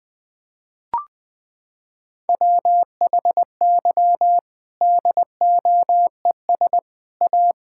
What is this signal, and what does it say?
Morse code, heard as a keyed tone, about 20 wpm, 700 Hz. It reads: WHY DOES A
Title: Sentences from Top 1000 Words - Review Part 5 - 20wpm